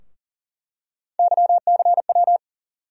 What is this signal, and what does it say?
Morse code, heard as a keyed tone, about 39 wpm, 700 Hz. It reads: YCW
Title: USA Call Sign Postfixes - 3 Letters 40wpm